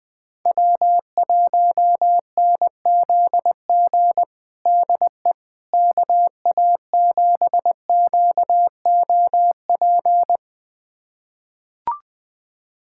Morse code, heard as a keyed tone, about 20 wpm, 700 Hz. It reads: W1NZG DE KA7QOP
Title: USA Callsign DE Callsign 20wpm